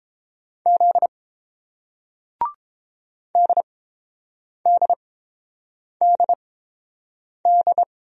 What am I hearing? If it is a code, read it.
Morse code, heard as a keyed tone, about 31 wpm, spaced as for 7 wpm, 700 Hz. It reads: Z DDDD